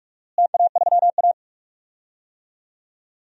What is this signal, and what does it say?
Morse code, heard as a keyed tone, about 46 wpm, 700 Hz. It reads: TA3A
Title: DXCC-100 Call Signs - 4 Characters - Repeated 2x - 45wpm